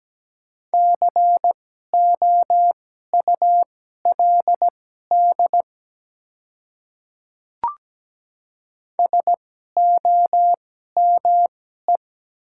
Morse code, heard as a keyed tone, about 17 wpm, spaced as for 12 wpm, 700 Hz. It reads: COULD SOME
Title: 2x Character Spacing - Top 100 Words - 17wpm